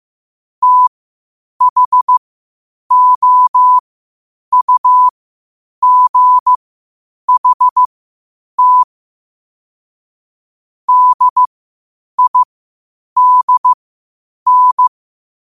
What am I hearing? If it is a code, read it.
Morse code, heard as a keyed tone, about 15 wpm, spaced as for 8 wpm, 1000 Hz. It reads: THOUGHT DIDN
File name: qotdfiles / 8wpm